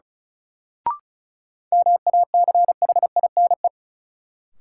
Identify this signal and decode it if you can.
Morse code, heard as a keyed tone, about 36 wpm, 700 Hz. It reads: MACHINE
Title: Sentences from Top 1000 Words - Review Part 2 - New Word Drill - 35wpm